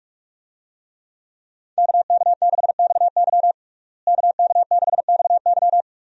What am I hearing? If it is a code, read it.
Morse code, heard as a keyed tone, about 46 wpm, 700 Hz. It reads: KK6XY KK6XY
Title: USA Call Signs - 5 Characters - Repeated 2x - 45wpm